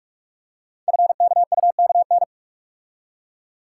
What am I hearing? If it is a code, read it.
Morse code, heard as a keyed tone, about 44 wpm, 700 Hz. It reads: FKUKN
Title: Code Groups - 5 Letters - 45wpm